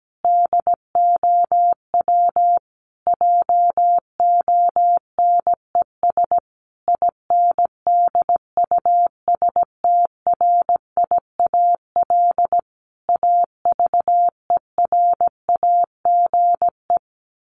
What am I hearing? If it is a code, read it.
Morse code, heard as a keyed tone, about 17 wpm, 700 Hz. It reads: DOW JONES INDUSTRIAL AVERAGE
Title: Sets of 5 Words 17wpm